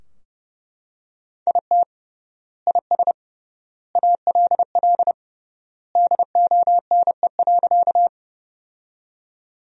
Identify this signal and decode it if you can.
Morse code, heard as a keyed tone, about 30 wpm, 700 Hz. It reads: IT IS ALL DONE.
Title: 3x Word-Spacing - Sentences from Top 500 Words - 30wpm